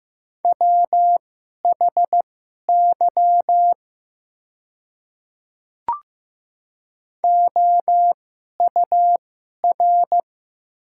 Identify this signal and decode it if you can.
Morse code, heard as a keyed tone, about 15 wpm, spaced as for 11 wpm, 690 Hz. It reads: WHY OUR
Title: 2x Character Spacing - Top 300 Words - 15wpm